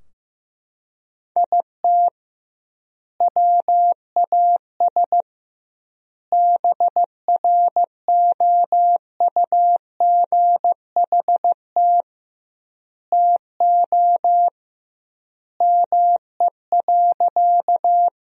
Morse code, heard as a keyed tone, about 15 wpm, 700 Hz. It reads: IT WAS BROUGHT TO ME.
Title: 2x Word-Spacing - Sentences from Top 500 Words - 15wpm